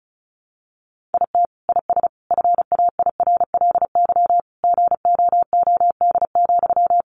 Morse code, heard as a keyed tone, about 34 wpm, 700 Hz. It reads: IT IS FAIRLY GOOD,
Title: Sentences from Top 1100 Words 35wpm